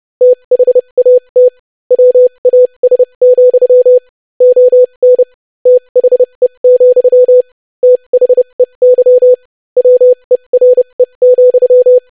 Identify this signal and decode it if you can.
Morse code, heard as a keyed tone, about 30 wpm, spaced as for 25 wpm, 500 Hz. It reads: THAT WAS, ON THE, THEY WERE,
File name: lesson / basicphrase-25wpm